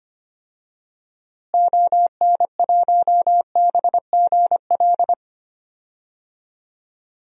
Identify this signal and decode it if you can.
Morse code, heard as a keyed tone, about 25 wpm, 700 Hz. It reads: ON1BGL
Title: DXCC-100 Call Signs - 6 Characters - Repeated 2x - 25wpm